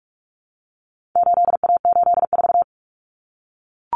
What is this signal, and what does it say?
Morse code, heard as a keyed tone, about 44 wpm, 700 Hz. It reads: 8A84
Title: Common Cut Numbers - 4-Digit 45wpm